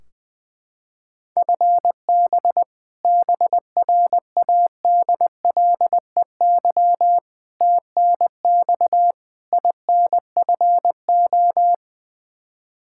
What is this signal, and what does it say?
Morse code, heard as a keyed tone, about 20 wpm, 700 Hz. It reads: FB BRADLEY TNX INFO
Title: Practice QSO - Exchange 3 - More Info - 20wpm